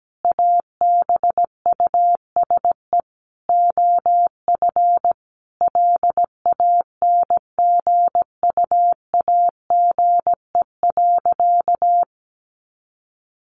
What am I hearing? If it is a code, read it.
Morse code, heard as a keyed tone, about 17 wpm, 700 Hz. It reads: ABUSE OF LANGUAGE.
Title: Sentences from Top 1300 Words 17wpm